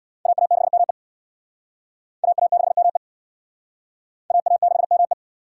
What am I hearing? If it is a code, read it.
Morse code, heard as a keyed tone, about 66 wpm, 700 Hz. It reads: AA6ME AA6ME AA6ME